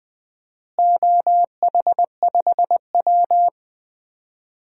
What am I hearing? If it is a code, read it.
Morse code, heard as a keyed tone, about 20 wpm, 700 Hz. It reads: OH5W
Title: DXCC-100 Call Signs - 4 Characters - Repeated 3x - 20wpm